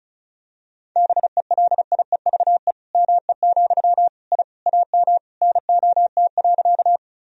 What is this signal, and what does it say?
Morse code, heard as a keyed tone, about 36 wpm, 700 Hz. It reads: BELIEVE ME, I AM NOT.
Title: Sentences from Top 1000 Words - Review Part 2 - 35wpm